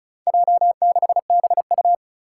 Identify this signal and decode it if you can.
Morse code, heard as a keyed tone, about 36 wpm, 700 Hz. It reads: J6BU
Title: Four Challenging Letter-Number Combinations 35wpm